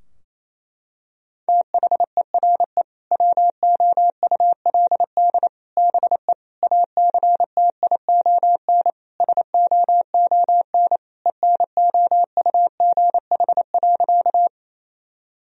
Morse code, heard as a keyed tone, about 28 wpm, 700 Hz. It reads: THERE WOULD BE ACTION SOON ENOUGH.
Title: Sentences from Top 1000 Words - Review Part 4 - 28wpm